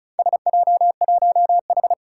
Morse code, heard as a keyed tone, about 36 wpm, 700 Hz. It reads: SJ1H
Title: Four Challenging Letter-Number Combinations 35wpm